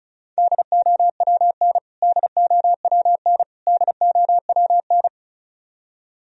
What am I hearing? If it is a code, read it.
Morse code, heard as a keyed tone, about 34 wpm, 700 Hz. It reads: DOWN DOWN DOWN
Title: Top 100 Words Repeated 35wpm